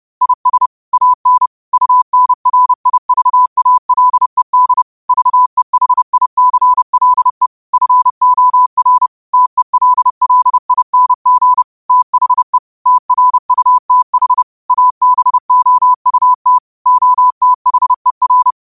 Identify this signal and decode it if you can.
Morse code, heard as a keyed tone, about 30 wpm, 1000 Hz. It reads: IS AN UNRIVALED VEHICLE FOR TELLING THE TRUTH ABOUT OTHER